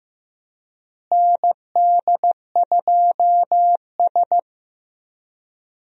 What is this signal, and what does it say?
Morse code, heard as a keyed tone, about 15 wpm, 700 Hz. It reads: ND2S
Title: USA Call Signs - 4 Characters - Repeated 3x - 15wpm